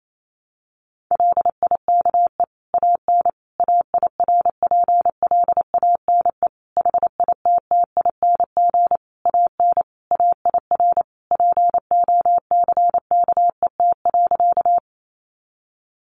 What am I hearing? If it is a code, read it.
Morse code, heard as a keyed tone, about 28 wpm, 700 Hz. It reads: LIKE AN AIRPLANE HITTING AN AIR POCKET.
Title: A Sound Of Thunder 28wpm